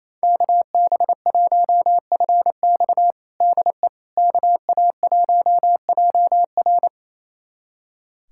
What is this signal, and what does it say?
Morse code, heard as a keyed tone, about 27 wpm, 700 Hz. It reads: KB1FX DE KA1JR
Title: USA Callsign DE Callsign - 5 Characters - 28wpm